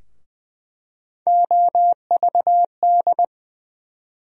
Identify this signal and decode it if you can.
Morse code, heard as a keyed tone, about 20 wpm, 700 Hz. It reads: OVD
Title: USA Call Sign Postfixes - 3 Letters 20wpm